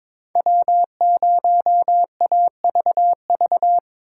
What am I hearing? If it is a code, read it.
Morse code, heard as a keyed tone, about 25 wpm, 700 Hz. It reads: W0AVV
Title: Speed Racing - USA 1x3 Call Signs 22wpm